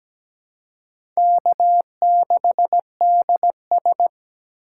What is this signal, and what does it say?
Morse code, heard as a keyed tone, about 17 wpm, 700 Hz. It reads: K6DS
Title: USA Call Signs - 4 Characters - 3x Character Spacing - 17wpm